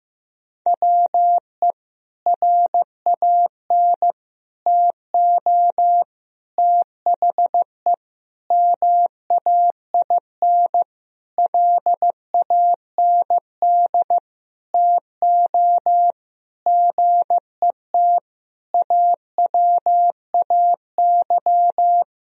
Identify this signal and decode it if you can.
Morse code, heard as a keyed tone, about 15 wpm, 700 Hz. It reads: WE RAN TO THE MAIN LAND TO GET AWAY
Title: Sentences from Top 1000 Words - Review Part 2 - 15wpm